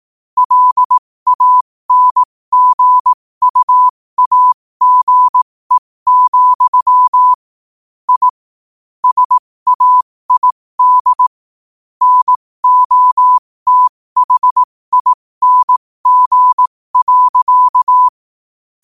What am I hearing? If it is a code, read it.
Morse code, heard as a keyed tone, about 18 wpm, spaced as for 15 wpm, 1000 Hz. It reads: LANGUAGE, I SAID NOTHING.
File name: qotdfiles / 15wpm